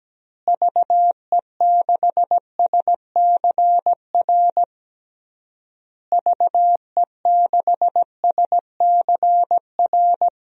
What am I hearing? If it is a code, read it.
Morse code, heard as a keyed tone, about 17 wpm, 700 Hz. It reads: VE6SCR VE6SCR
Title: DXCC-100 Call Signs - 6 Characters - Repeated 3x - 17wpm